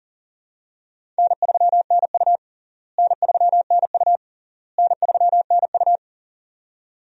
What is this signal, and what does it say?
Morse code, heard as a keyed tone, about 40 wpm, 700 Hz. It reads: N3NU N3NU N3NU
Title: USA Call Signs - 4 Characters - Repeated 3x - 40wpm